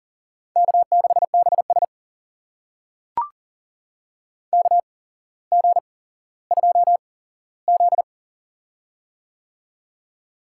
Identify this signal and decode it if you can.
Morse code, heard as a keyed tone, about 42 wpm, spaced as for 17 wpm, 700 Hz. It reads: K6BS KG2Z
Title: USA Call Signs - 4 Characters - 8x Character Spacing - 40wpm